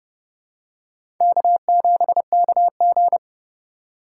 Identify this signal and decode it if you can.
Morse code, heard as a keyed tone, about 32 wpm, 700 Hz. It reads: K7KG
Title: USA Call Signs - 4 Characters - 3x Character Spacing - 30wpm